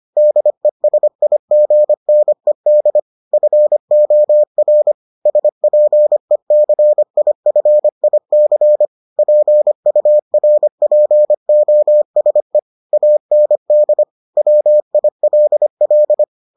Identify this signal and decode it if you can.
Morse code, heard as a keyed tone, about 25 wpm, 600 Hz. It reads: DESIGNED FOR SPECIFIC PURPOSE AND WILL